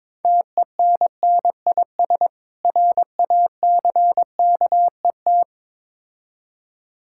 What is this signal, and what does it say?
Morse code, heard as a keyed tone, about 23 wpm, 700 Hz. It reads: TENNIS RACKET
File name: practice / 22wpm / Difficult Words with Hints - 22wpm